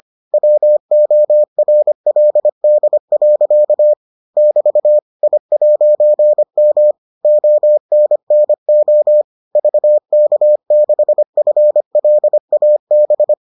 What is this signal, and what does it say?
Morse code, heard as a keyed tone, about 25 wpm, 600 Hz. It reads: WORLD. <BT> I'M ONNO VK6FLAB